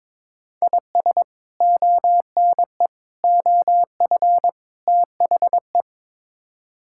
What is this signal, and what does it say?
Morse code, heard as a keyed tone, about 22 wpm, 700 Hz. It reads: IS ONE OF THE
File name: practice / 22wpm / Build A Phrase - 5-Word Phrases - 22wpm